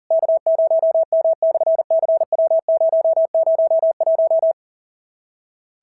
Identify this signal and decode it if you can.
Morse code, heard as a keyed tone, about 41 wpm, 650 Hz. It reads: K0M/CW001
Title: SOTA QSO Practice 40wpm